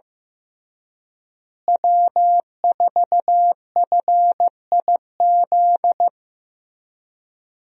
Morse code, heard as a keyed tone, about 15 wpm, 700 Hz. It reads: W4FIZ
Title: USA Call Signs - 5 Characters - Repeated 2x - 15wpm